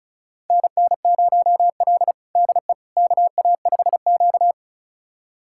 Morse code, heard as a keyed tone, about 36 wpm, 700 Hz. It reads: NN0L DE KA5Q
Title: USA Callsign DE Callsign - 4 Characters - 35wpm